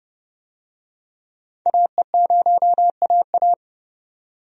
Morse code, heard as a keyed tone, about 30 wpm, 700 Hz. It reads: AE0AA
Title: USA Call Signs - 5 Characters - 30wpm